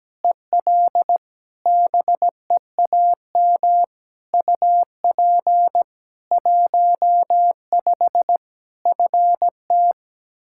Morse code, heard as a keyed tone, about 17 wpm, 700 Hz. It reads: EL BEAM UP 15 FT
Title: K1USN SST QSO Practice 17wpm